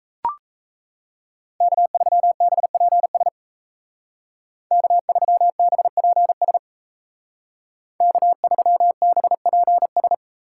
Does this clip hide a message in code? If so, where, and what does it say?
Yes: Morse code from 1.6 to 10.1 s, heard as a keyed tone, about 36 wpm, 700 Hz. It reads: K3BPS K3BPS K3BPS